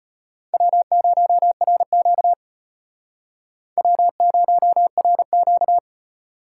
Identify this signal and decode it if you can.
Morse code, heard as a keyed tone, about 35 wpm, 700 Hz. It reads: W0RQ W0RQ